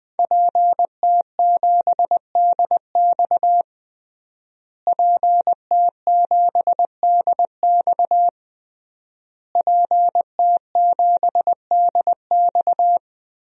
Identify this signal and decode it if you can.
Morse code, heard as a keyed tone, about 20 wpm, 700 Hz. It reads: PT7DX PT7DX PT7DX
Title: DXCC-100 Call Signs - 5 Characters - Repeated 3x - 20wpm